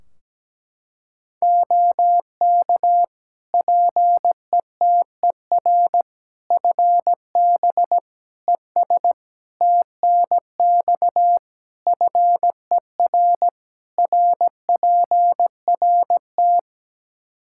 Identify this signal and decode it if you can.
Morse code, heard as a keyed tone, about 17 wpm, 700 Hz. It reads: OK PETER FB ES TNX FER RPRT
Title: K1USN SST QSO Practice 17wpm